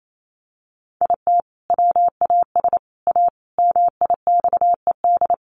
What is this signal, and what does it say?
Morse code, heard as a keyed tone, about 28 wpm, 700 Hz. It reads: IT WAS A MIXED
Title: Sentences from Top 1500 Words 28wpm